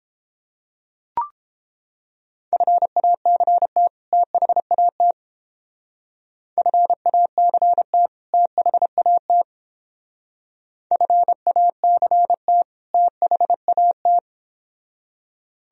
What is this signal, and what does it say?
Morse code, heard as a keyed tone, about 29 wpm, 700 Hz. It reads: FACT THAT FACT THAT FACT THAT